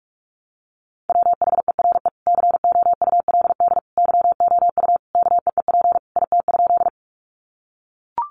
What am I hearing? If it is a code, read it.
Morse code, heard as a keyed tone, about 44 wpm, 700 Hz. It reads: WHERE COULD YOU KEEP IT?
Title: Sentences from Top 300 Words 45wpm